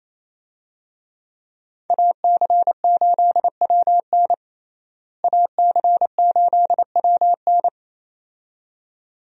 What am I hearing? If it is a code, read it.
Morse code, heard as a keyed tone, about 27 wpm, 700 Hz. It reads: AC8WN AC8WN
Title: USA Call Signs - 5 Characters - Repeated 2x - 28wpm